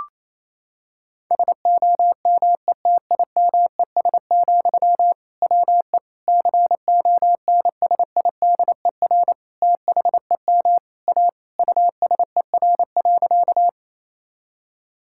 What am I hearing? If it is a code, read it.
Morse code, heard as a keyed tone, about 28 wpm, 700 Hz. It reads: SOMETIMES, WE CONSIDER THEM A USER.
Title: Sentences from Top 1000 Words - Review Part 5 - 28wpm